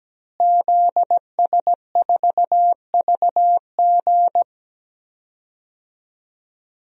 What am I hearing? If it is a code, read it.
Morse code, heard as a keyed tone, about 17 wpm, 700 Hz. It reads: ZS4VG
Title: DXCC-100 Call Signs - 5 Characters - Repeated 3x - 17wpm